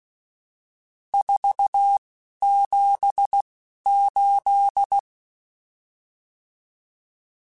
Morse code, heard as a keyed tone, about 16 wpm, spaced as for 11 wpm, 800 Hz. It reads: 478
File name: RSTs / RST-mp3s / Various RST @10